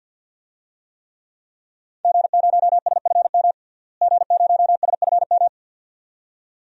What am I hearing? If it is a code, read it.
Morse code, heard as a keyed tone, about 53 wpm, 700 Hz. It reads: G0SFM G0SFM
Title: DXCC-100 Call Signs - 5 Characters - Repeated 2x - 50wpm